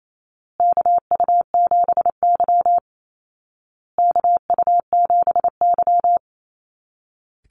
Morse code, heard as a keyed tone, about 28 wpm, 700 Hz. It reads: KU7Y KU7Y
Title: Call Signs Repeated 2x - 28wpm